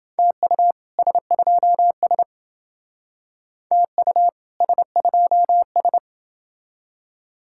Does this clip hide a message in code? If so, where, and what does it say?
Yes: Morse code from 0.2 to 6.0 s, heard as a keyed tone, about 30 wpm, 700 Hz. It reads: TU S2S TU S2S